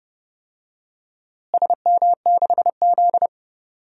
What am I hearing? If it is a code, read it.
Morse code, heard as a keyed tone, about 32 wpm, 700 Hz. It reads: SM6Z
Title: DXCC-100 Call Signs - 4 Characters - Repeated 2x - 30wpm